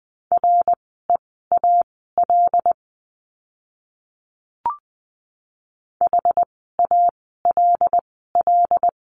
Morse code, heard as a keyed tone, about 20 wpm, spaced as for 14 wpm, 700 Hz. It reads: REAL HALL
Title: 075 Lesson 13f - Letter L - Words 20wpm